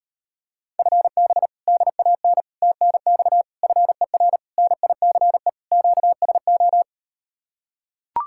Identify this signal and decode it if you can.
Morse code, heard as a keyed tone, about 39 wpm, 700 Hz. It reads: FB DAN TNX FER NICE QSO